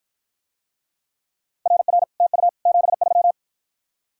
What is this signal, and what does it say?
Morse code, heard as a keyed tone, about 53 wpm, 700 Hz. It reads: RR TU 73